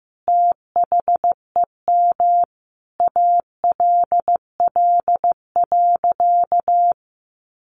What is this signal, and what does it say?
Morse code, heard as a keyed tone, about 15 wpm, 700 Hz. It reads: THEM ALL.
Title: Sentences from Top 1400 Words 15wpm